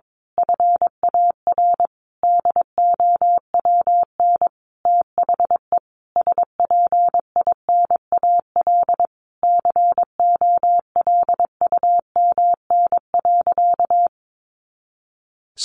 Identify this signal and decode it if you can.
Morse code, heard as a keyed tone, about 22 wpm, 700 Hz. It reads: FAR DOWN THE SPINAL COLUMN.
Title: A Sound Of Thunder 22wpm